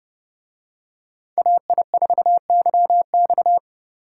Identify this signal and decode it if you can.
Morse code, heard as a keyed tone, about 27 wpm, 700 Hz. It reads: AI4YX